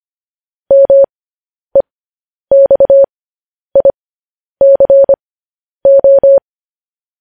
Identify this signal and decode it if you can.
Morse code, heard as a keyed tone, about 25 wpm, spaced as for 10 wpm, 550 Hz. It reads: MEXICO
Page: News Headlines in Morse Code at 10 WPM